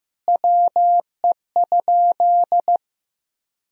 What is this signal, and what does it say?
Morse code, heard as a keyed tone, about 16 wpm, 700 Hz. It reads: WE?
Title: Sentences from Top 200 Words and Abbreviations 15wpm